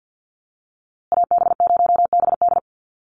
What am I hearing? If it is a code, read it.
Morse code, heard as a keyed tone, about 50 wpm, 700 Hz. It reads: AB0BD
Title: USA Call Signs Repeated 2x 50wpm